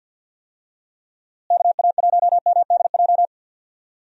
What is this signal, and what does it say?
Morse code, heard as a keyed tone, about 51 wpm, 700 Hz. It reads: KA1MDJ
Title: USA Call Signs - 6 Characters - Repeated 3x - 50wpm